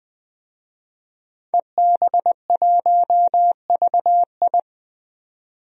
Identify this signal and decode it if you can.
Morse code, heard as a keyed tone, about 21 wpm, 700 Hz. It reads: EB1VI
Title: DXCC-100 Call Signs - 5 Characters - 20wpm